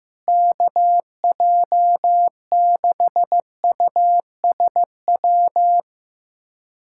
Morse code, heard as a keyed tone, about 14 wpm, 700 Hz. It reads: KJ6USW
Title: USA Callsign DE Callsign - 6 Characters - 15wpm